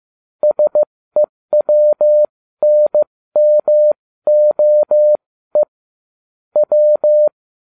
Morse code, heard as a keyed tone, about 15 wpm, spaced as for 13 wpm, 600 Hz. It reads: SEWNMOE W